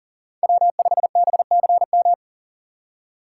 Morse code, heard as a keyed tone, about 50 wpm, 700 Hz. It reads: W5BCM
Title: Speed Racing - USA Call Signs 40wpm